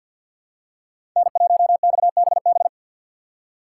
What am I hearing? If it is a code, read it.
Morse code, heard as a keyed tone, about 53 wpm, 700 Hz. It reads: N1XBB
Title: USA Call Signs - 5 Characters - 2x Character Spacing - 50wpm